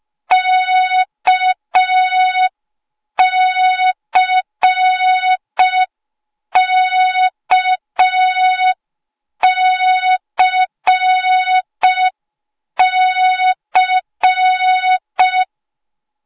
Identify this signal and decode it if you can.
Morse code, heard as a keyed tone, about 5 wpm, 3700 Hz. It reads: KCKCC